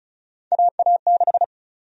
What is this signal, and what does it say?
Morse code, heard as a keyed tone, about 36 wpm, 700 Hz. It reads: AA6
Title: USA Call Sign Prefixes - 35wpm